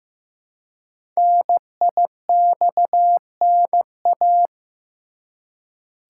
Morse code, heard as a keyed tone, about 15 wpm, 700 Hz. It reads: NIXNA
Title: Code Groups - 5 Letters - 15wpm